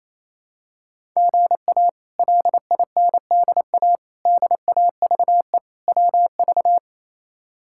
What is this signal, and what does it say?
Morse code, heard as a keyed tone, about 28 wpm, 700 Hz. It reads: GA LINDA DAVE WV
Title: K1USN SST QSO Practice - 28wpm